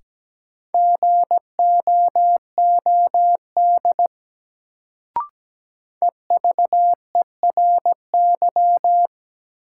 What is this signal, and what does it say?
Morse code, heard as a keyed tone, about 17 wpm, 700 Hz. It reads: GOOD EVERY
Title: Sentences from Top 200 Words - New Word Drill - 17wpm